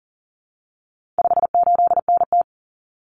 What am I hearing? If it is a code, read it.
Morse code, heard as a keyed tone, about 37 wpm, 700 Hz. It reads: 58NT
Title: Common Cut Numbers - 4-Digit 40wpm